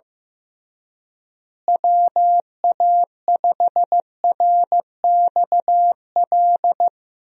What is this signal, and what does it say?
Morse code, heard as a keyed tone, about 16 wpm, 700 Hz. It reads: WA5RXL
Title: USA Call Signs - 6 Characters - Repeated 3x - 15wpm